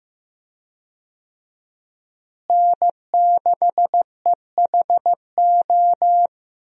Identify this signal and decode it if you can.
Morse code, heard as a keyed tone, about 16 wpm, 700 Hz. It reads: N6EHO
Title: USA Call Signs - 5 Characters - Repeated 2x - 15wpm